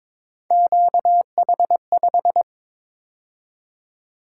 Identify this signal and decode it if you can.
Morse code, heard as a keyed tone, about 26 wpm, 700 Hz. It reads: QH5
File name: Speed-Racing - Three Letter-Number Combinations - 22wpm